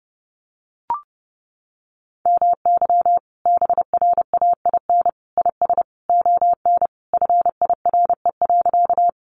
Morse code, heard as a keyed tone, about 30 wpm, 700 Hz. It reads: MY BRAIN IS ON FIRE.
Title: Sentences from Top 1100 Words 30wpm